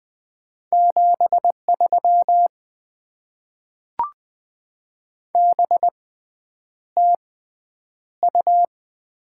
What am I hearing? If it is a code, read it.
Morse code, heard as a keyed tone, about 20 wpm, 700 Hz. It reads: 7 3 B T U
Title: Top 20 QSO Elements - 6x Character Spacing - 20wpm